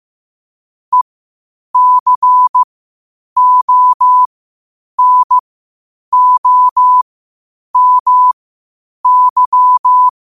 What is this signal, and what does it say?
Morse code, heard as a keyed tone, about 15 wpm, spaced as for 8 wpm, 1000 Hz. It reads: ECONOMY
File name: qotdfiles / 8wpm